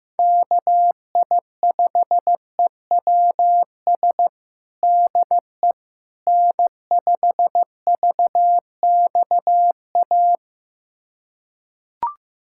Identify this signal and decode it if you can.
Morse code, heard as a keyed tone, about 15 wpm, 700 Hz. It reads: KI5EWS DE N5VXA
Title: USA Callsign DE Callsign 15wpm